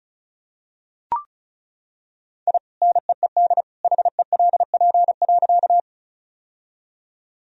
Morse code, heard as a keyed tone, about 35 wpm, 700 Hz. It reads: I NEED HELP.
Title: Sentences from Top 1000 Words - Review Part 1 - 35wpm